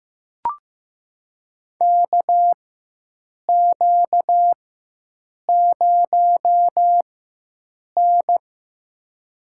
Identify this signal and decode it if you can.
Morse code, heard as a keyed tone, about 15 wpm, spaced as for 7 wpm, 700 Hz. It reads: KQ0N
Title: USA Call Signs - 4 Characters - 4x Character Spacing - 15wpm